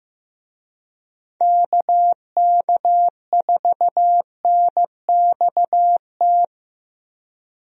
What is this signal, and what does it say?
Morse code, heard as a keyed tone, about 15 wpm, 700 Hz. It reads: KK4NXT
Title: USA Call Signs - 6 Characters - Repeated 2x - 15wpm